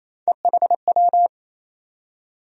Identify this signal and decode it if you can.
Morse code, heard as a keyed tone, about 28 wpm, 700 Hz. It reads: EHW
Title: USA Call Sign Postfixes - 3 Letters 28wpm